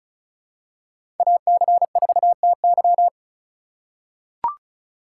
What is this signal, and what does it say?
Morse code, heard as a keyed tone, about 35 wpm, 700 Hz. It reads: AC4TY
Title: USA Call Signs - 5 Characters - 2x Character Spacing - 35wpm